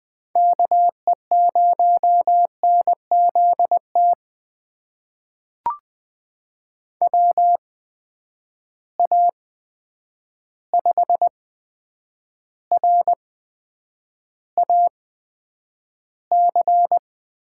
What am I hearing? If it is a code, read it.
Morse code, heard as a keyed tone, about 20 wpm, spaced as for 9 wpm, 700 Hz. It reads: KE0NZT WA5RAC
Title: USA Call Signs - 6 Characters - 8x Character Spacing - 20wpm